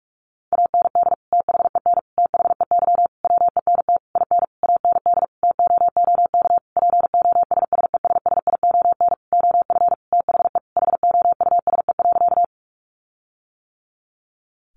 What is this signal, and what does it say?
Morse code, heard as a keyed tone, about 44 wpm, 700 Hz. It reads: AND THEN THEY WENT IN AND TOOK POSSESSION OF THE HOUSE.
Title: Sentences from Top 1400 Words 45wpm